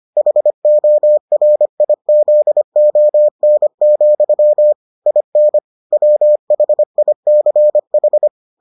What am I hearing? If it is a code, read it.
Morse code, heard as a keyed tone, about 25 wpm, 600 Hz. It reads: HORIZON, IN WHICH